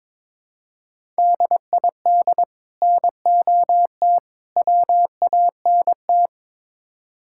Morse code, heard as a keyed tone, about 22 wpm, 700 Hz. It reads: DID NOT WANT
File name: practice / 22wpm / Build A Phrase - 5-Word Phrases - 22wpm